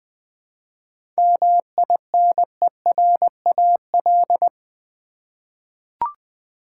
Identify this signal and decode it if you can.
Morse code, heard as a keyed tone, about 20 wpm, 700 Hz. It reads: MINERAL
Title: 7 Letter Words - First 3 Letters Given - 20wpm